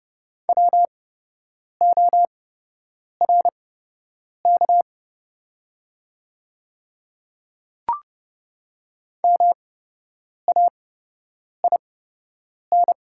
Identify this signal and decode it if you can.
Morse code, heard as a keyed tone, about 30 wpm, spaced as for 8 wpm, 690 Hz. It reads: WORK MAIN
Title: 8x Character Spacing - Top 500 Words - 30wpm